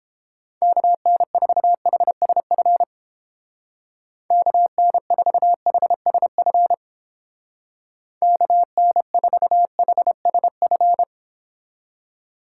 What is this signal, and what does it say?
Morse code, heard as a keyed tone, about 31 wpm, 700 Hz. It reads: KN4HSF KN4HSF KN4HSF